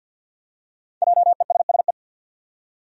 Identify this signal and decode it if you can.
Morse code, heard as a keyed tone, about 47 wpm, 700 Hz. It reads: JESSE
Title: USA Common Names - 50wpm